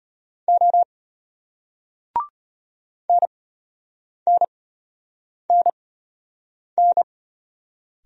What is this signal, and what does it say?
Morse code, heard as a keyed tone, about 33 wpm, spaced as for 8 wpm, 700 Hz. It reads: O NNNN